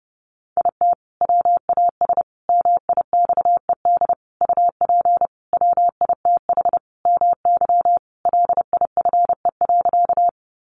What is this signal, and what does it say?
Morse code, heard as a keyed tone, about 30 wpm, 700 Hz. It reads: IT WAS MIXED UP WITH MY LIFE.
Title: Sentences from Top 1500 Words 30wpm